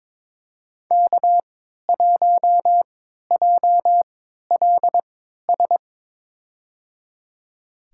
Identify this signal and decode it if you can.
Morse code, heard as a keyed tone, about 23 wpm, spaced as for 13 wpm, 700 Hz. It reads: K1JLS
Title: USA Call Signs - 5 Characters - 3x Character Spacing - 22wpm